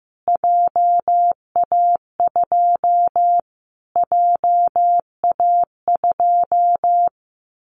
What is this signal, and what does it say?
Morse code, heard as a keyed tone, about 15 wpm, 700 Hz. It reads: JA2 JA2
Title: Callsign Prefixes for Top 25 DXCC - Repeated 3x - 15wpm